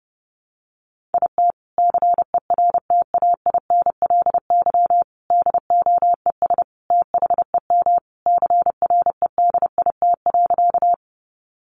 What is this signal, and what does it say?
Morse code, heard as a keyed tone, about 30 wpm, 700 Hz. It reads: IT CERTAINLY DOES THEM CREDIT.
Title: Sentences from Top 1000 Words 30wpm